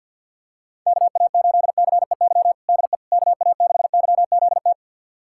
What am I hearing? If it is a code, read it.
Morse code, heard as a keyed tone, about 47 wpm, 690 Hz. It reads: KA8CEY DE KA6YZT
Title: USA Callsign DE Callsign - 6 Characters - 50wpm